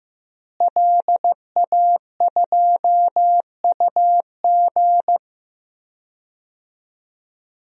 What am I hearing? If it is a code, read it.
Morse code, heard as a keyed tone, about 15 wpm, 700 Hz. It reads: LA2UG
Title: DXCC-100 Call Signs - 5 Characters - Repeated 2x - 15wpm